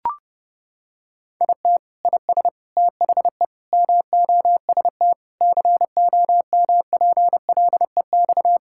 Morse code, heard as a keyed tone, about 32 wpm, 700 Hz. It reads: IT IS THE MOST COMPLEX
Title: Sentences from Top 1000 Words - Review Part 6 - 30wpm